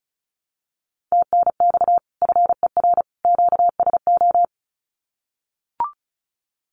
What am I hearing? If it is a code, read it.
Morse code, heard as a keyed tone, about 34 wpm, 700 Hz. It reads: TNX FER QSO
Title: Common QSO Phrases 35wpm